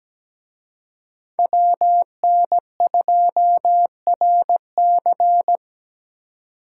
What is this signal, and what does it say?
Morse code, heard as a keyed tone, about 20 wpm, 700 Hz. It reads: WN2RC
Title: Speed Racing - USA 2x2 Call Signs 17wpm